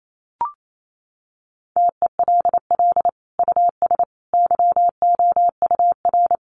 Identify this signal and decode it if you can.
Morse code, heard as a keyed tone, about 28 wpm, 700 Hz. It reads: TELL US YOUR
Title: Sentences from Top 1400 Words 28wpm